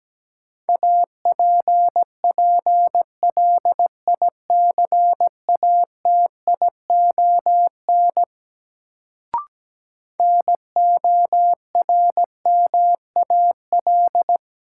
Morse code, heard as a keyed tone, about 17 wpm, 700 Hz. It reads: APPLICATION NORMAL
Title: Sentences from Top 1200 Words - New Word Drill - 17wpm